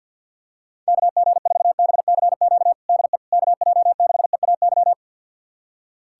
Morse code, heard as a keyed tone, about 52 wpm, 700 Hz. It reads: KG4BCQ DE KJ6EAY
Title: USA Callsign DE Callsign - 6 Characters - 50wpm